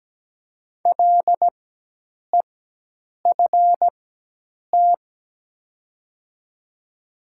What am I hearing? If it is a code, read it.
Morse code, heard as a keyed tone, about 17 wpm, spaced as for 8 wpm, 700 Hz. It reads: LEFT
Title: 4x Character Spacing - Top 500 Words - 17wpm